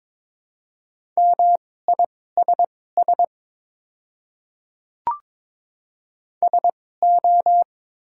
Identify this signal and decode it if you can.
Morse code, heard as a keyed tone, about 22 wpm, spaced as for 16 wpm, 700 Hz. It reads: MISS SO
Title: 2x Character Spacing - Top 500 Words - 22wpm